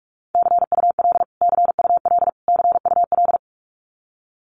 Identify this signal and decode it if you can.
Morse code, heard as a keyed tone, about 46 wpm, 700 Hz. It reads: CUL CUL CUL
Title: Abbreviations Repeated 3x - 45wpm